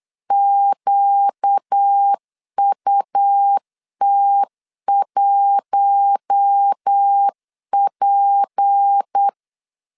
Morse code, heard as a keyed tone, about 8 wpm, 800 Hz. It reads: QUT1P
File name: LEZIONI / 6 PKX0 / 6 P 8wpm 3025